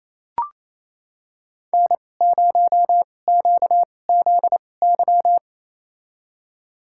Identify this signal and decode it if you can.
Morse code, heard as a keyed tone, about 28 wpm, spaced as for 20 wpm, 700 Hz. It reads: N0QZY
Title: USA Call Signs - 5 Characters - 2x Character Spacing - 28wpm